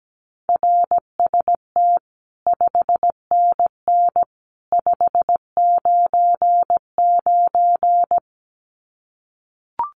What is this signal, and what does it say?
Morse code, heard as a keyed tone, about 17 wpm, 700 Hz. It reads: RST 5NN 599
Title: Common QSO Phrases 17wpm